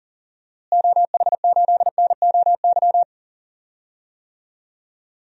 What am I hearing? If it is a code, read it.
Morse code, heard as a keyed tone, about 39 wpm, 700 Hz. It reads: OH8NOY
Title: DXCC-100 Call Signs - 6 Characters - Repeated 3x - 40wpm